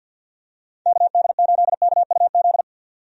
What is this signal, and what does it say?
Morse code, heard as a keyed tone, about 46 wpm, 700 Hz. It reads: KD8KUZ
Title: USA Call Signs - 6 Characters - 50wpm